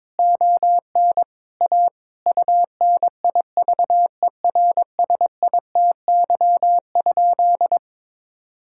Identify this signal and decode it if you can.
Morse code, heard as a keyed tone, about 22 wpm, 700 Hz. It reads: ON A UNIVERSITY?
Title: Sentences from Top 1000 Words - Review Part 5 - 22wpm